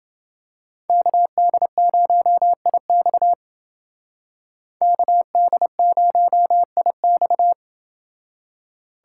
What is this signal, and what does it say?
Morse code, heard as a keyed tone, about 28 wpm, 700 Hz. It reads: KD0IX KD0IX